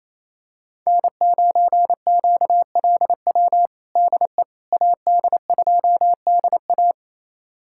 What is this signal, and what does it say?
Morse code, heard as a keyed tone, about 28 wpm, 700 Hz. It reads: N9QLW DE AD2DA
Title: USA Callsign DE Callsign - 5 Characters - 28wpm